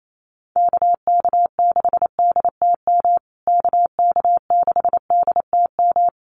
Morse code, heard as a keyed tone, about 28 wpm, 700 Hz. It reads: KK6DTM KK6DTM
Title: USA Call Signs Repeated 3x 28wpm